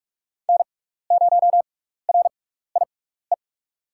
Morse code, heard as a keyed tone, about 44 wpm, spaced as for 15 wpm, 700 Hz. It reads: N0RIE
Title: USA Call Signs - 5 Characters - 6x Character Spacing - 45wpm